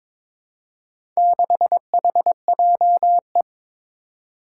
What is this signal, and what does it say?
Morse code, heard as a keyed tone, about 22 wpm, 700 Hz. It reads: 6HJE
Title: Four Challenging Letter-Number Combinations 22wpm